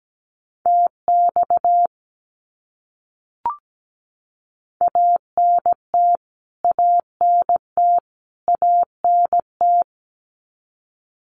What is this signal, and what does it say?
Morse code, heard as a keyed tone, about 17 wpm, 700 Hz. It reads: TX ANT ANT ANT
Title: Abbreviations Repeated 3x - 17wpm